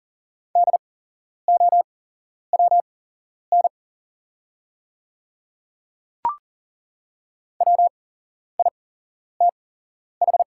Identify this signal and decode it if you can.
Morse code, heard as a keyed tone, about 41 wpm, spaced as for 11 wpm, 700 Hz. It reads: DOWN WITH
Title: 8x Character Spacing - Top 100 Words - 40wpm